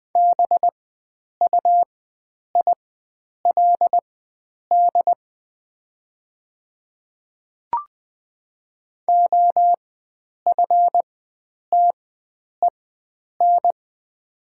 Morse code, heard as a keyed tone, about 20 wpm, spaced as for 9 wpm, 700 Hz. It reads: BUILD OFTEN
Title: 4x Character Spacing - Top 300 Words - 20wpm